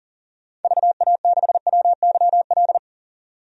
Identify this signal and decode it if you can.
Morse code, heard as a keyed tone, about 40 wpm, 700 Hz. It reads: VA6WYL
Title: DXCC-100 Call Signs - 6 Characters - 40wpm